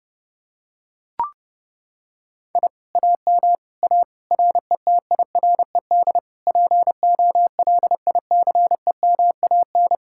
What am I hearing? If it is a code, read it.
Morse code, heard as a keyed tone, about 29 wpm, 700 Hz. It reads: I AM A RETIRED POLICEMAN